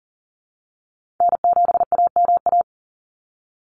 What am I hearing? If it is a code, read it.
Morse code, heard as a keyed tone, about 40 wpm, 700 Hz. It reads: N7AMA
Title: NA SOTA Chaser Callsigns 40wpm